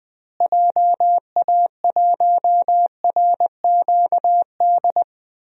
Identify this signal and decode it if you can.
Morse code, heard as a keyed tone, about 20 wpm, 700 Hz. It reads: JA1RQD
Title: DXCC-100 Call Signs - 6 Characters - Repeated 3x - 20wpm